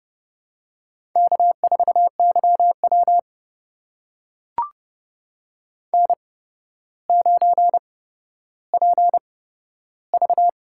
Morse code, heard as a keyed tone, about 30 wpm, spaced as for 13 wpm, 700 Hz. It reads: K4YW N9PV
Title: USA Call Signs - 4 Characters - 8x Character Spacing - 30wpm